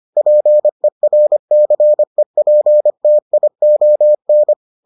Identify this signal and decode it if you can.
Morse code, heard as a keyed tone, about 25 wpm, 600 Hz. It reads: PERCEPTION